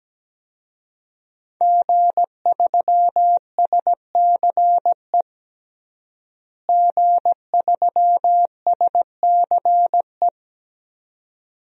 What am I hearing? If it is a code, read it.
Morse code, heard as a keyed tone, about 17 wpm, 700 Hz. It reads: G3SCE G3SCE
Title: DXCC-100 Call Signs - 5 Characters - Repeated 2x - 17wpm